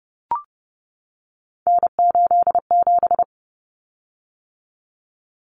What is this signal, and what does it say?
Morse code, heard as a keyed tone, about 30 wpm, 700 Hz. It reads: N87
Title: Common Cut Numbers - 3-Digit 30wpm